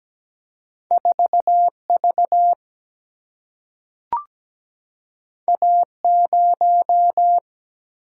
Morse code, heard as a keyed tone, about 17 wpm, 700 Hz. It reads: 4 V A 0
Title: Single-Letter Single-Number 17wpm